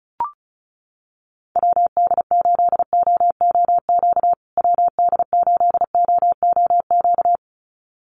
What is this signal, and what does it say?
Morse code, heard as a keyed tone, about 34 wpm, 700 Hz. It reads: WD8OOQ WD8OOQ
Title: USA Call Signs Repeated 2x 35wpm